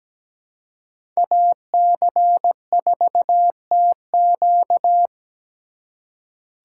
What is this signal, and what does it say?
Morse code, heard as a keyed tone, about 17 wpm, 700 Hz. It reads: AC4TQ
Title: USA Call Signs - 5 Characters - 2x Character Spacing - 17wpm